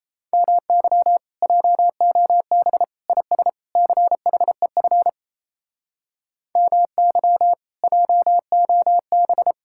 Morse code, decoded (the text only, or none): MY JOB IS CHEF MY JOB